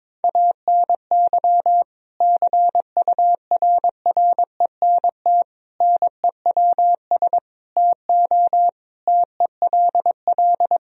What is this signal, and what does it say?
Morse code, heard as a keyed tone, about 22 wpm, 700 Hz. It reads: ANY CURRENT NEWS TO TELL
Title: Sentences from Top 1000 Words - Review Part 4 - 22wpm